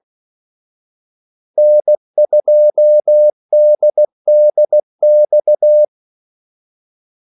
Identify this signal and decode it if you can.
Morse code, heard as a keyed tone, about 16 wpm, 600 Hz. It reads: N2DDX
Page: Kaldesignaler 16 wpm